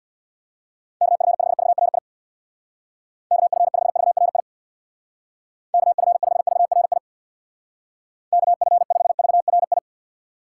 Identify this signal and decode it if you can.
Morse code, heard as a keyed tone, about 63 wpm, 700 Hz. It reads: KF5VRI KF5VRI KF5VRI KF5VRI